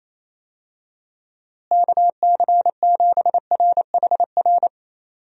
Morse code, heard as a keyed tone, about 28 wpm, 700 Hz. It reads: KC7RHR
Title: USA Call Signs - 6 Characters - Repeated 3x - 28wpm